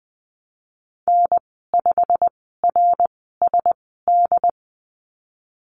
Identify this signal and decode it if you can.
Morse code, heard as a keyed tone, about 20 wpm, spaced as for 15 wpm, 700 Hz. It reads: N5RSD
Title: 088 Lesson 15g - Number 5 - Call Signs 20wpm